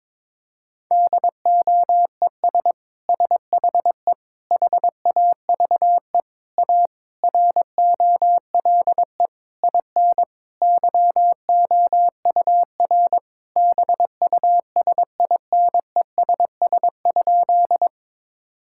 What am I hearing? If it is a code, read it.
Morse code, heard as a keyed tone, about 22 wpm, 700 Hz. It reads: DOES SHE HAVE A ROLE IN YOUR BUSINESS?
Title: Sentences from Top 1000 Words - Review Part 6 - 22wpm